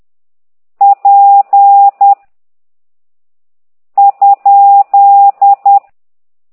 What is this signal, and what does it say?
Morse code, heard as a keyed tone, about 10 wpm, 800 Hz. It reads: P ?
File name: ct2idw / exerc8